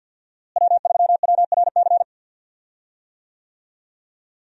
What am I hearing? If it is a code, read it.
Morse code, heard as a keyed tone, about 50 wpm, 700 Hz. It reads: W3WRC
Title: USA Call Signs - 5 Characters - Repeated 2x - 50wpm